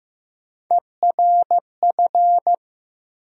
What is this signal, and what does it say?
Morse code, heard as a keyed tone, about 15 wpm, 700 Hz. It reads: ERF
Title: USA Call Sign Postfixes - 3 Letters 15wpm